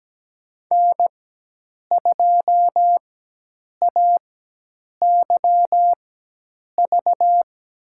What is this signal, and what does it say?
Morse code, heard as a keyed tone, about 17 wpm, spaced as for 8 wpm, 700 Hz. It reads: N2AYV
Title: USA Call Signs - 4x Character Spacing - 17wpm